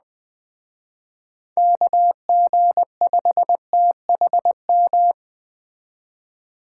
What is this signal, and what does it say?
Morse code, heard as a keyed tone, about 20 wpm, 700 Hz. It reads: KG5THM
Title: Speed Racing - USA 2x3 Call Signs 17wpm